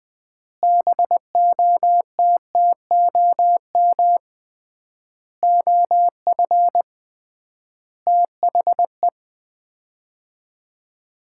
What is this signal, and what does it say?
Morse code, heard as a keyed tone, about 21 wpm, 700 Hz. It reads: BOTTOM OF THE
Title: 3x Word-Spacing - Sets of 5 Words - 20wpm